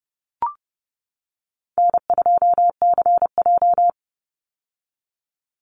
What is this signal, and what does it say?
Morse code, heard as a keyed tone, about 30 wpm, 700 Hz. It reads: N2CJ
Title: Easy Call Signs - 30wpm